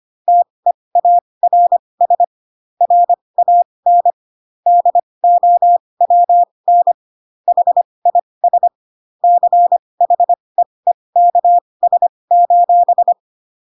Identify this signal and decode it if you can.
Morse code, heard as a keyed tone, about 25 wpm, spaced as for 20 wpm, 700 Hz. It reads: TEARS RAN DOWN HIS CHEEKS:
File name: Pinocchio / Pinocchio ch7 15 25 ik1hge